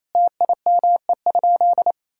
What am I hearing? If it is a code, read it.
Morse code, heard as a keyed tone, about 28 wpm, 700 Hz. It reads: TIME?
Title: Sentences from Top 1000 Words - Review Part 5 - 28wpm